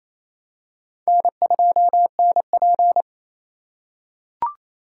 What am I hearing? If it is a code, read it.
Morse code, heard as a keyed tone, about 28 wpm, 700 Hz. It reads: N2NP
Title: USA Call Signs - 4 Characters - 3x Character Spacing - 28wpm